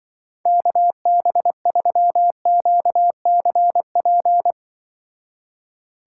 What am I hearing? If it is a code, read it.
Morse code, heard as a keyed tone, about 23 wpm, 700 Hz. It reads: KB3QCP
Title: Speed Racing - USA 2x3 Call Signs 20wpm